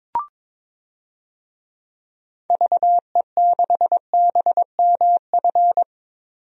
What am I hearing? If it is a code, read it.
Morse code, heard as a keyed tone, about 21 wpm, 700 Hz. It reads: VE6BMF
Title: DXCC-100 Call Signs - 6 Characters - 22wpm